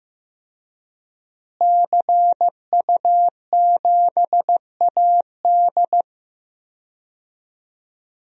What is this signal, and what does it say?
Morse code, heard as a keyed tone, about 15 wpm, 700 Hz. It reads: CU7AD
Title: DXCC-100 Call Signs - 5 Characters - Repeated 3x - 15wpm